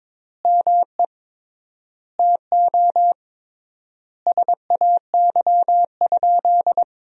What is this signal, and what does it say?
Morse code, heard as a keyed tone, about 22 wpm, 700 Hz. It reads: ME TO SAY?
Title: 3x Word-Spacing - Sentences from Top 200 Words - 22wpm